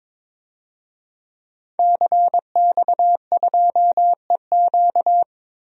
Morse code, heard as a keyed tone, about 22 wpm, 700 Hz. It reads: CX2EQ
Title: DXCC-100 Call Signs - 5 Characters - 22wpm